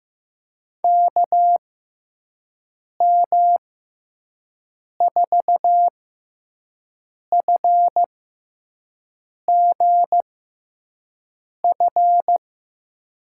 Morse code, heard as a keyed tone, about 15 wpm, spaced as for 5 wpm, 700 Hz. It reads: KM4FGF
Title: USA Call Signs - 6 Characters - 6x Character Spacing - 15wpm